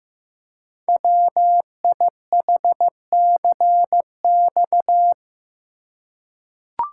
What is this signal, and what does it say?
Morse code, heard as a keyed tone, about 15 wpm, 700 Hz. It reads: WIHCX
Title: Code Groups - 5 Letters - 15wpm